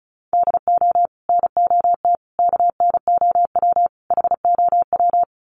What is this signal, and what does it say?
Morse code, heard as a keyed tone, about 34 wpm, 700 Hz. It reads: DO NOT KNOW HOW
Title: Sets of 4 Words 35wpm